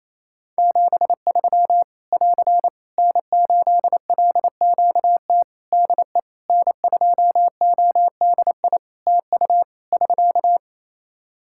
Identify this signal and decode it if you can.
Morse code, heard as a keyed tone, about 28 wpm, 700 Hz. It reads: 73 <AR> N8LQT DE N2ODI TU <SK>
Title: K1USN SST QSO Practice 28wpm